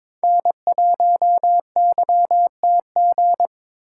Morse code, heard as a keyed tone, about 22 wpm, 700 Hz. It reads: N1YTG
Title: USA Call Signs - 5 Characters - 6x Character Spacing - 22wpm